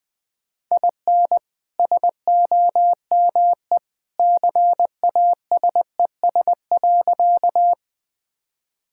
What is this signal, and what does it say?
Morse code, heard as a keyed tone, about 21 wpm, 700 Hz. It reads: IN SOME CASES.
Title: Sentences with 2-Word Phrases - 20wpm